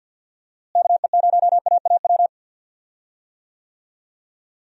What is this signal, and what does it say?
Morse code, heard as a keyed tone, about 49 wpm, 700 Hz. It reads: KE0AAW
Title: USA Call Signs - 6 Characters - Repeated 2x - 50wpm